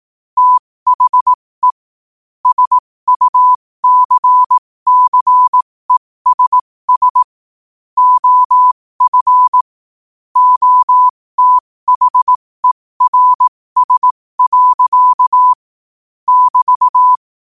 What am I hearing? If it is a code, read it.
Morse code, heard as a keyed tone, about 18 wpm, spaced as for 15 wpm, 1000 Hz. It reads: THE SUCCESS OF OTHERS. <BT>
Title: 15 WPM morse code quotes for Thu, 29 Feb 2024